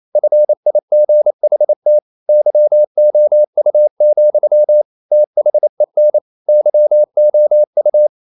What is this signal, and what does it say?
Morse code, heard as a keyed tone, about 28 wpm, 600 Hz. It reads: FIGHT YOU, THEN YOU
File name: CW / Practice / EnglishQuotations0228